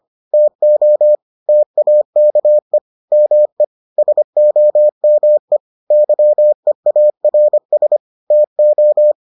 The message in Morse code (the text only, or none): TO TAKE ME SOME YEARS TO